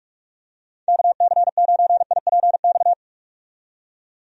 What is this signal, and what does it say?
Morse code, heard as a keyed tone, about 46 wpm, 700 Hz. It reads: KC9IPX
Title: USA Call Signs - 6 Characters - 45wpm